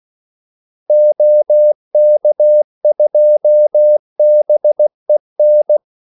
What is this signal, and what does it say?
Morse code, heard as a keyed tone, about 16 wpm, 600 Hz. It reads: OK2BEN
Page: Kaldesignaler 16 wpm